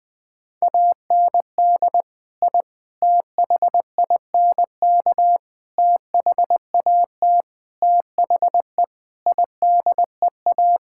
Morse code, heard as a keyed tone, about 20 wpm, 700 Hz. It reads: AND I THINK THAT THE IDEA
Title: Sentences with 5-Word Phrases - 20wpm